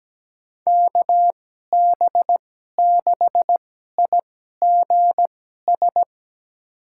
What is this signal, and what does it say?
Morse code, heard as a keyed tone, about 17 wpm, spaced as for 12 wpm, 700 Hz. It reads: KB6IGS
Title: USA Call Signs - 6 Characters - 2x Character Spacing - 17wpm